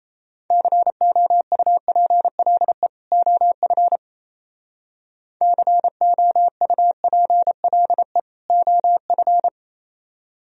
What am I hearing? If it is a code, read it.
Morse code, heard as a keyed tone, about 37 wpm, 700 Hz. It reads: COUPLE OF COUPLE OF